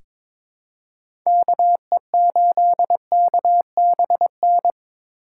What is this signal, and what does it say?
Morse code, heard as a keyed tone, about 22 wpm, 700 Hz. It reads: KE8KBN
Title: USA Call Signs - 6 Characters - 2x Character Spacing - 22wpm